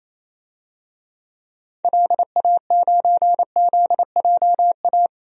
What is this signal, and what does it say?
Morse code, heard as a keyed tone, about 28 wpm, 700 Hz. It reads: LA9ZJA
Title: DXCC-100 Call Signs - 6 Characters - 28wpm